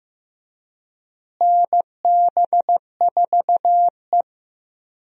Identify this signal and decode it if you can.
Morse code, heard as a keyed tone, about 16 wpm, 700 Hz. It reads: NB4E
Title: USA Call Signs - 4 Characters - Repeated 2x - 15wpm